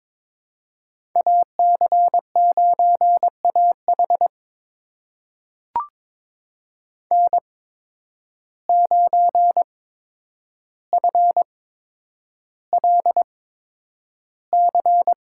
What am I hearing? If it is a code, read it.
Morse code, heard as a keyed tone, about 22 wpm, spaced as for 9 wpm, 700 Hz. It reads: AC9AH N9FLC
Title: USA Call Signs - 5 Characters - 8x Character Spacing - 22wpm